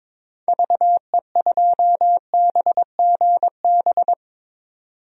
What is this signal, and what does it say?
Morse code, heard as a keyed tone, about 21 wpm, 700 Hz. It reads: VE2BGB
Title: DXCC-100 Call Signs - 6 Characters - 22wpm